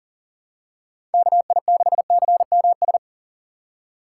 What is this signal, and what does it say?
Morse code, heard as a keyed tone, about 37 wpm, 700 Hz. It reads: KI6CMS
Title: USA Call Signs - 2x Character Spacing - 40wpm